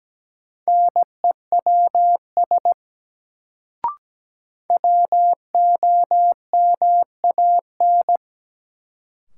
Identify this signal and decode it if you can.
Morse code, heard as a keyed tone, about 17 wpm, 700 Hz. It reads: NEWS WOMAN
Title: Sets of 5 Words - New Word Drill - 17wpm